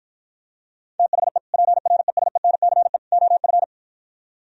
Morse code, heard as a keyed tone, about 52 wpm, 700 Hz. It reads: THE PRESENCE OF